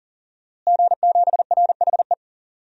Morse code, heard as a keyed tone, about 40 wpm, 700 Hz. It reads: G7RHE
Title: DXCC-100 Call Signs - 5 Characters - Repeated 3x - 40wpm